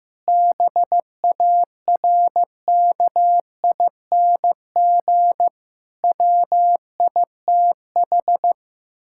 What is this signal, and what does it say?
Morse code, heard as a keyed tone, about 15 wpm, 700 Hz. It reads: BARKING WITH
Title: Pangram Challenge v2 - 15wpm